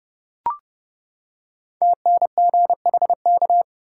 Morse code, heard as a keyed tone, about 29 wpm, 700 Hz. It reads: TNGHK
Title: Code Groups - 5 Letter:Numbers - 30wpm